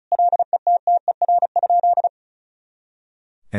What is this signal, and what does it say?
Morse code, heard as a keyed tone, about 36 wpm, 700 Hz. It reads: LETTER?
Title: Sentences from Top 300 Words and Abbreviations 35wpm